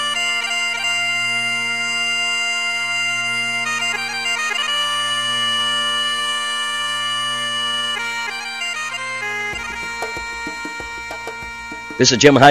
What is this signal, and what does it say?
Dial-up modem handshake, with speech.